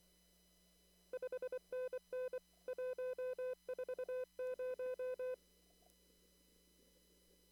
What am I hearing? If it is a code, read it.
Morse code, heard as a keyed tone, about 24 wpm, 510 Hz. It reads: 5NN 140
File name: CW / kt5j wpx cw-op-k5pi